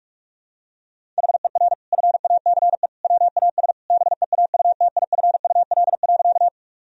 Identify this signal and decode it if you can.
Morse code, heard as a keyed tone, about 46 wpm, 700 Hz. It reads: HER FACE WAS BEAUTIFUL.
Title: Sentences from Top 1000 Words - Review Part 2 - 45wpm